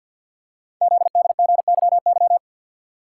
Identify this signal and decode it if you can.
Morse code, heard as a keyed tone, about 50 wpm, 700 Hz. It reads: ZDGYY
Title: Code Groups - 5 Letters - 50wpm